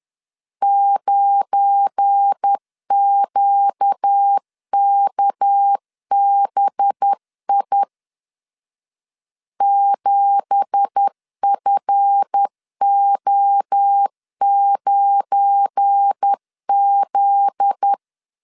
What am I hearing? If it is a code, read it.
Morse code, heard as a keyed tone, about 10 wpm, 800 Hz. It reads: 9QKBI 7FO9Z
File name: CW / DET / 1 9 1500C S B 10wpm 4439